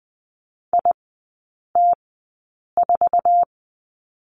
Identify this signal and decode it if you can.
Morse code, heard as a keyed tone, about 20 wpm, spaced as for 8 wpm, 700 Hz. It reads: IT4
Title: 047 Lesson 9e - Number 4 - Three Characters 20wpm